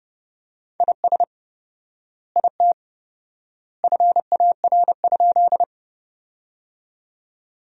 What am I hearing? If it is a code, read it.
Morse code, heard as a keyed tone, about 30 wpm, 700 Hz. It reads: IS IT FAR?
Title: 4x Word-Spacing - Sentences from Top 300 Words - 30wpm